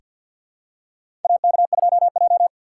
Morse code, heard as a keyed tone, about 46 wpm, 700 Hz. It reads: AK2J
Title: USA Call Signs - 4x Character Spacing - 50wpm